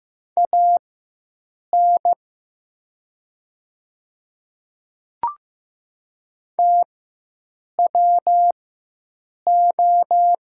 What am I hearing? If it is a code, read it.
Morse code, heard as a keyed tone, about 15 wpm, spaced as for 7 wpm, 700 Hz. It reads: AN TWO
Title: 4x Character Spacing - Top 200 Words - 15wpm